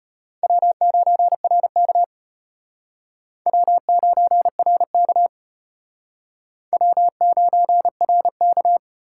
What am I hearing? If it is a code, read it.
Morse code, heard as a keyed tone, about 32 wpm, 700 Hz. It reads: W9RK W9RK W9RK